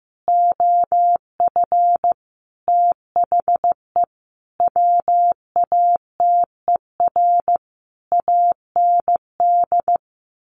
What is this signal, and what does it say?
Morse code, heard as a keyed tone, about 15 wpm, 700 Hz. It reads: OF THE WATER AND
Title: Sentences from Top 1300 Words 15wpm